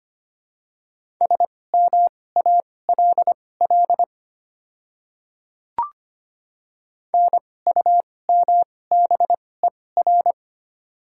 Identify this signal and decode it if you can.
Morse code, heard as a keyed tone, about 25 wpm, spaced as for 18 wpm, 700 Hz. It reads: SMALL NUMBER
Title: 2x Character Spacing - Top 200 Words - 25wpm